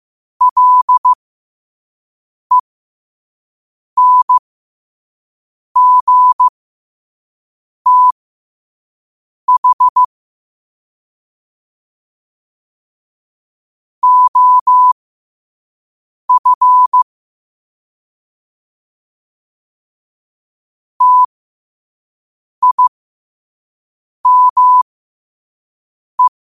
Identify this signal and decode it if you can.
Morse code, heard as a keyed tone, about 15 wpm, spaced as for 5 wpm, 1000 Hz. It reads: LENGTH OF TIME
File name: qotdfiles / 5wpm